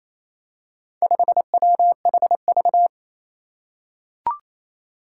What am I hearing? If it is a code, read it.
Morse code, heard as a keyed tone, about 28 wpm, 700 Hz. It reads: 5WHV
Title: Four Challenging Letter-Number Combinations 28wpm